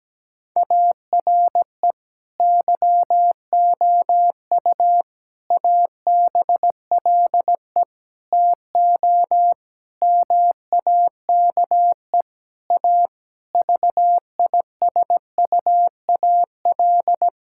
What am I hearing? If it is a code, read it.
Morse code, heard as a keyed tone, about 17 wpm, 700 Hz. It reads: ARE YOU ABLE TO MAKE A VISUAL
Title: Sentences from Top 1000 Words - Review Part 6 - 17wpm